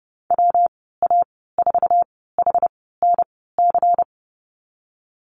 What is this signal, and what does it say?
Morse code, heard as a keyed tone, about 30 wpm, spaced as for 17 wpm, 700 Hz. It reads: WA4HNC
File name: Lesson 37g - Letter x - Call Signs 30wpm